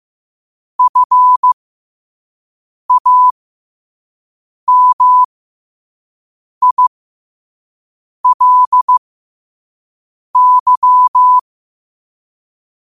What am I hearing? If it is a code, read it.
Morse code, heard as a keyed tone, about 15 wpm, spaced as for 5 wpm, 1000 Hz. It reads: FAMILY